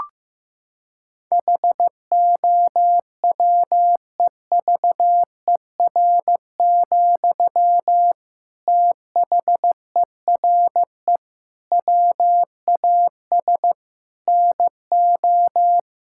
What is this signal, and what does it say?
Morse code, heard as a keyed tone, about 15 wpm, 700 Hz. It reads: HOWEVER, THERE WAS NO
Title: Sentences from Top 1600 Words 15wpm